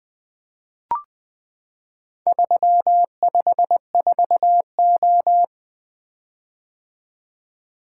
Morse code, heard as a keyed tone, about 20 wpm, 700 Hz. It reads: 354O
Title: Four Letter-Number Combinations - 20wpm